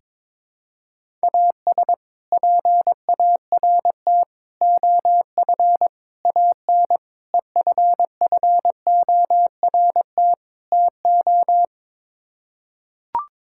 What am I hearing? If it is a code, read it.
Morse code, heard as a keyed tone, about 22 wpm, 700 Hz. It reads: AS PART OF AN EFFORT TO
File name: practice / 22wpm / Sets of 6 Words - 22wpm